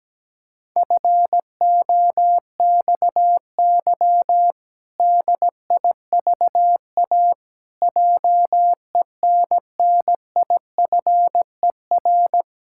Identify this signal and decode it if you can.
Morse code, heard as a keyed tone, about 17 wpm, 700 Hz. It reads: FOXY DIVA JENNIFER
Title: Pangram Challenge v2 - 17wpm